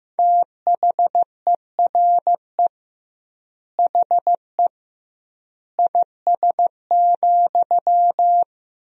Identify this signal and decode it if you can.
Morse code, heard as a keyed tone, about 15 wpm, 700 Hz. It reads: THERE HE IS,
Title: 2x Word-Spacing - Sentences from Top 100 Words - 15wpm